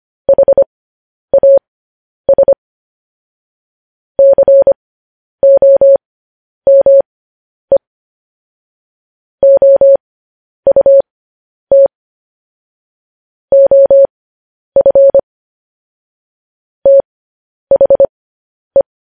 Morse code, heard as a keyed tone, about 25 wpm, spaced as for 10 wpm, 550 Hz. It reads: HAS COME OUT OF THE